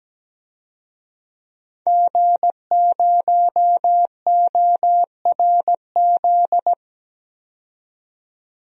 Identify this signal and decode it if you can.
Morse code, heard as a keyed tone, about 17 wpm, 700 Hz. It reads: G0ORZ
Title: DXCC-100 Call Signs - 5 Characters - 17wpm